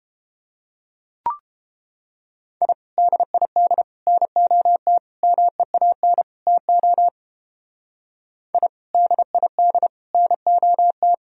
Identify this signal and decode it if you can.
Morse code, heard as a keyed tone, about 29 wpm, 700 Hz. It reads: I DID NOT MEAN TO I DID NOT